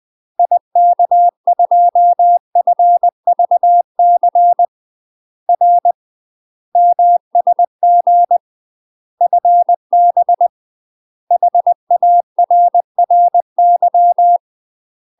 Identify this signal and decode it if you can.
Morse code, heard as a keyed tone, about 20 wpm, 700 Hz. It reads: IK2FVC R MSG FB HARRY
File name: UK MASTER QSOs TRAINING / 20WPM / UK MASTER QSOs 0017